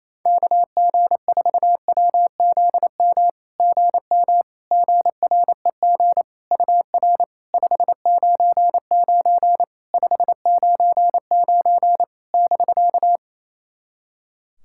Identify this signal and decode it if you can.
Morse code, heard as a keyed tone, about 28 wpm, 700 Hz. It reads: KG4WZM GM GREG UR 599 599 <BK>
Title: SOTA QSO Practice 28wpm